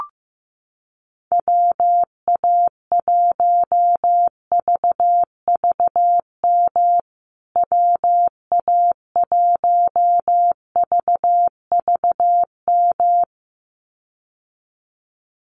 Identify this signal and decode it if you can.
Morse code, heard as a keyed tone, about 14 wpm, 700 Hz. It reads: WA1VVM WA1VVM
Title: USA Call Signs Repeated 2x 15wpm